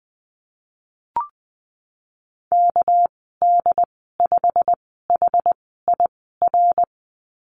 Morse code, heard as a keyed tone, about 20 wpm, spaced as for 14 wpm, 700 Hz. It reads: KD5HIR
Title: 219 Lesson 35g - Number 0 - Call Signs 20wpm